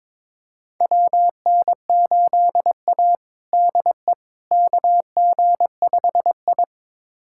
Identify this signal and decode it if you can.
Morse code, heard as a keyed tone, about 22 wpm, 700 Hz. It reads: WN8A DE KG5I
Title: USA Callsign DE Callsign - 4 Characters - 22wpm